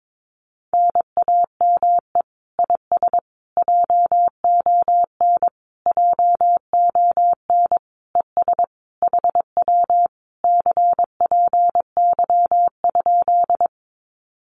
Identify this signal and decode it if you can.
Morse code, heard as a keyed tone, about 22 wpm, 700 Hz. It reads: NAME IS JON JON ES HW CPY?
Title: Common QSO Phrases Encore 22wpm